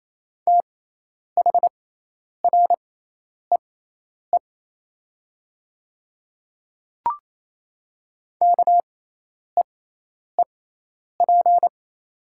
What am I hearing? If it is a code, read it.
Morse code, heard as a keyed tone, about 28 wpm, spaced as for 10 wpm, 700 Hz. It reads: THREE KEEP
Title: 6x Character Spacing - Top 300 Words - 28wpm